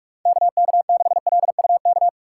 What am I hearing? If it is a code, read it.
Morse code, heard as a keyed tone, about 46 wpm, 690 Hz. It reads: KK6LUK
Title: USA Call Signs - 6 Characters - 3x Character Spacing - 45wpm